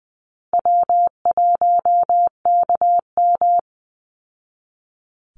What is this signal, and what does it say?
Morse code, heard as a keyed tone, about 20 wpm, 700 Hz. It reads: W1KM
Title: Call Signs Repeated 2x - 20wpm